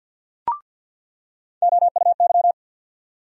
Morse code, heard as a keyed tone, about 53 wpm, 700 Hz. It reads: OUY
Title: USA Call Sign Postfixes - 3 Letters 50wpm